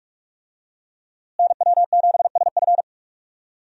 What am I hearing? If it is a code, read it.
Morse code, heard as a keyed tone, about 43 wpm, 700 Hz. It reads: NW7SF
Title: USA Call Signs - 5 Characters - Repeated 3x - 45wpm